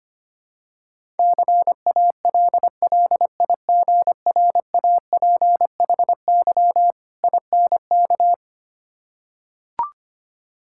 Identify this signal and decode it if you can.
Morse code, heard as a keyed tone, about 25 wpm, 700 Hz. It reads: CALLIGRAPHY INK
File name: practice / 25wpm / Difficult Words with Hints - 25wpm